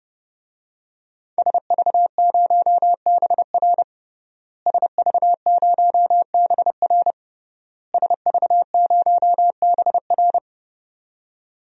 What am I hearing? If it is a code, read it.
Morse code, heard as a keyed tone, about 30 wpm, 700 Hz. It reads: SV0BR SV0BR SV0BR
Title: DXCC-100 Call Signs - 5 Characters - Repeated 3x - 30wpm